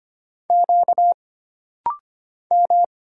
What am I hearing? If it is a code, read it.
Morse code, heard as a keyed tone, about 25 wpm, 700 Hz. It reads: Q M
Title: Single Letters - Rapid-Fire 25wpm